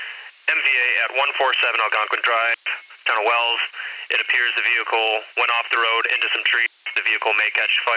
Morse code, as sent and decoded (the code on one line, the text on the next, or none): none